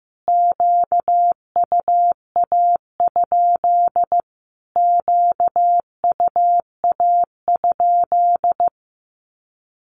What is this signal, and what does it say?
Morse code, heard as a keyed tone, about 15 wpm, 700 Hz. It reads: QUA? QUA?
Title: Q-Codes Repeated 3x 15wpm